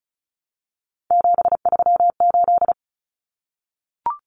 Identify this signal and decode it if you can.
Morse code, heard as a keyed tone, about 34 wpm, 700 Hz. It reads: 738
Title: Numbers - 3-Digit 35wpm